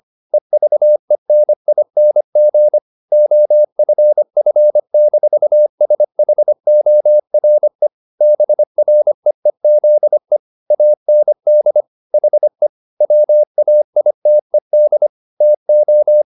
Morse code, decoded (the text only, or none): EVENING OFF-SHORE BREEZE AND HE WAITED TO